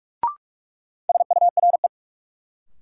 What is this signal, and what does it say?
Morse code, heard as a keyed tone, about 42 wpm, 700 Hz. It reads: SURE
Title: Sentences from Top 300 Words - New Word Drill - 45wpm